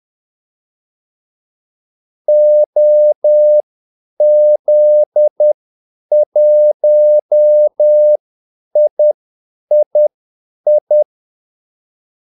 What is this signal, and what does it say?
Morse code, heard as a keyed tone, about 10 wpm, spaced as for 8 wpm, 600 Hz. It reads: OZ1III
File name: Callsigns-10-8wpm